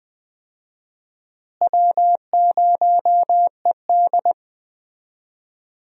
Morse code, heard as a keyed tone, about 21 wpm, 700 Hz. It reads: W0ED
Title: USA Call Signs - 4 Characters - Repeated 3x - 20wpm